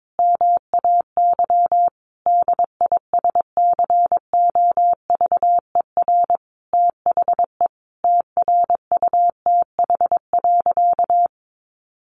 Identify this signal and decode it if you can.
Morse code, heard as a keyed tone, about 22 wpm, 700 Hz. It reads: MAY DISCOVER THE TRUTH.
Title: Sentences from Top 1100 Words 22wpm